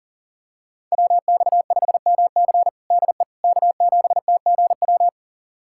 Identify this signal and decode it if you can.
Morse code, heard as a keyed tone, about 40 wpm, 700 Hz. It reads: WX5MC DE K7TGW
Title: USA Callsign DE Callsign - 5 Characters - 40wpm